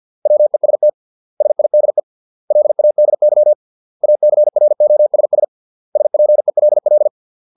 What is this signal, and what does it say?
Morse code, heard as a keyed tone, about 50 wpm, 600 Hz. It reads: WEST SIDE LADY ACROSS SPELL